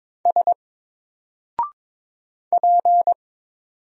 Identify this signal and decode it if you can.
Morse code, heard as a keyed tone, about 22 wpm, 700 Hz. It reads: S P
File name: Single-Letter-Number-22wpm